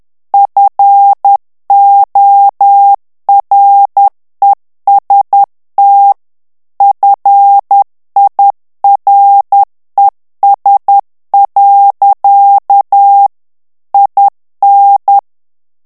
Morse code, heard as a keyed tone, about 10 wpm, 800 Hz. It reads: FOREST FIRES. IN
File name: ANGLAIS texte clair 2 vitesse 10 mots minute